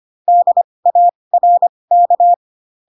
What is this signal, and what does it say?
Morse code, heard as a keyed tone, about 25 wpm, spaced as for 20 wpm, 700 Hz. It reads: DARK
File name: Pinocchio / Pinocchio ch6 15 25 ik1hge